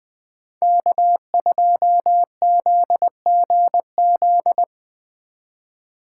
Morse code, heard as a keyed tone, about 20 wpm, 700 Hz. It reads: K2ZGZ